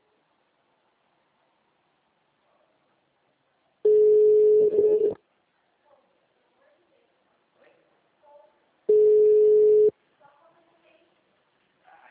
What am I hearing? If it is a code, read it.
Ringback tone (Europe, 425 Hz), with speech.